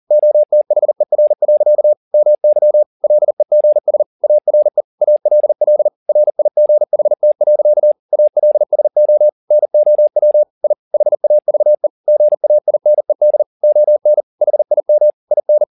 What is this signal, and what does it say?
Morse code, heard as a keyed tone, about 40 wpm, 600 Hz. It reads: OTHER. MY LEGS ARE ALL RIGHT. ALSO NOW I HAVE GAINED ON HIM IN
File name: the old man and the sea 2 40wpm